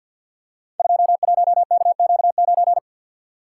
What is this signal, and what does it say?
Morse code, heard as a keyed tone, about 50 wpm, 700 Hz. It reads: 21KQ9
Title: Code Groups - 5 Letter:Numbers - 50wpm